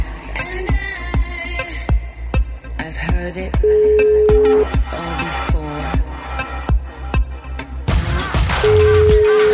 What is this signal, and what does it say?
Ringback tone (Europe, 425 Hz).